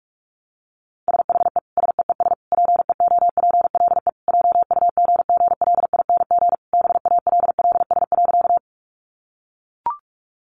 Morse code, heard as a keyed tone, about 44 wpm, 700 Hz. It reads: SHE SEES PEOPLE JUGGLING BALLS.
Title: Easy Phrases 45wpm